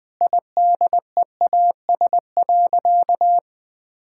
Morse code, heard as a keyed tone, about 20 wpm, 700 Hz. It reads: IDEAS.
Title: Sentences with 2-Word Phrases - 20wpm